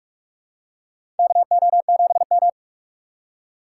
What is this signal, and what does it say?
Morse code, heard as a keyed tone, about 46 wpm, 700 Hz. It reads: KO7M
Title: USA Call Signs - 4 Characters - 4x Character Spacing - 45wpm